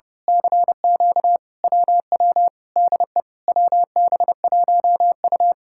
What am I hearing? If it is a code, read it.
Morse code, heard as a keyed tone, about 30 wpm, 700 Hz. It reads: CQ WW DE WB1U
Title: Speed-Racing Common QSO Phrases - 30wpm